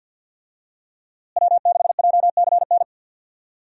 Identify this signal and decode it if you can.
Morse code, heard as a keyed tone, about 46 wpm, 700 Hz. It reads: W6JCN
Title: USA Call Signs - 5 Characters - Repeated 2x - 50wpm